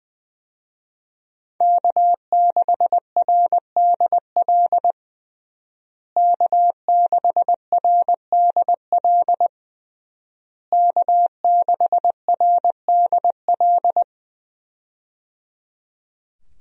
Morse code, heard as a keyed tone, about 20 wpm, 700 Hz. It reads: K6RDL K6RDL K6RDL
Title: USA Call Signs - 5 Characters - Repeated 3x - 20wpm